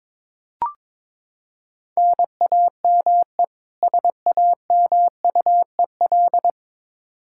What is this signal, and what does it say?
Morse code, heard as a keyed tone, about 22 wpm, 700 Hz. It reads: NAME SAMUEL
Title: Practice QSO - Exchange 1 - Beginning - 22wpm